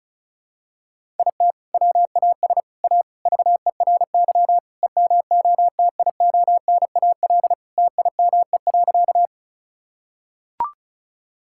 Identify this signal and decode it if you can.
Morse code, heard as a keyed tone, about 34 wpm, 700 Hz. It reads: IT WAS A VERY EMOTIONAL TIME.
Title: Sentences from Top 1000 Words - Review Part 5 - 35wpm